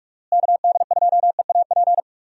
Morse code, heard as a keyed tone, about 46 wpm, 700 Hz. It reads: KD2EAP
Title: USA Call Signs - 6 Characters - Repeated 2x - 45wpm